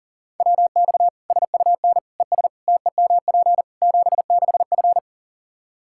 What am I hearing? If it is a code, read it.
Morse code, heard as a keyed tone, about 40 wpm, 700 Hz. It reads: WX SUN ES TEMP 76F
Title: Practice QSO - Exchange 2 - Info - 40wpm